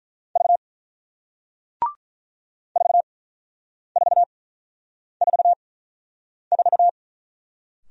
Morse code, heard as a keyed tone, about 48 wpm, spaced as for 9 wpm, 690 Hz. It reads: V 4444